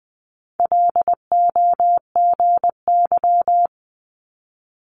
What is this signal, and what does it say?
Morse code, heard as a keyed tone, about 20 wpm, 700 Hz. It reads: LOGY
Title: Multi-letter Phonemes Quick Fire 20wpm